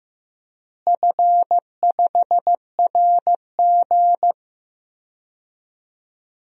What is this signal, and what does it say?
Morse code, heard as a keyed tone, about 15 wpm, 700 Hz. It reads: F5RG
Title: DXCC-100 Call Signs - 4 Characters - Repeated 3x - 15wpm